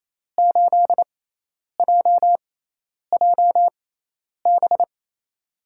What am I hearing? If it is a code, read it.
Morse code, heard as a keyed tone, about 28 wpm, spaced as for 10 wpm, 700 Hz. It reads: 8JJB
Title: USA Call Signs - 6 Characters - 6x Character Spacing - 28wpm